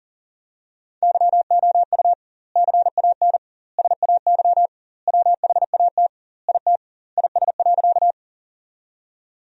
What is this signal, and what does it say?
Morse code, heard as a keyed tone, about 40 wpm, 700 Hz. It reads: YOU CAN SAY WHAT IT IS.
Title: 2x Word-Spacing - Sentences from Top 200 Words - 40wpm